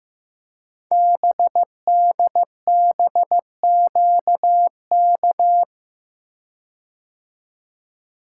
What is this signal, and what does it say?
Morse code, heard as a keyed tone, about 15 wpm, 700 Hz. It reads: BDBQK
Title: Code Groups - 5 Letters - 15wpm